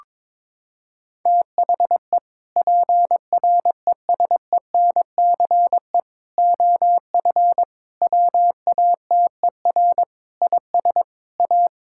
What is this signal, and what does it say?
Morse code, heard as a keyed tone, about 22 wpm, 700 Hz. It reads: THE PRESENCE OF WATER IS A
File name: practice / 22wpm / Sentences with 3-Word Phrases - 22wpm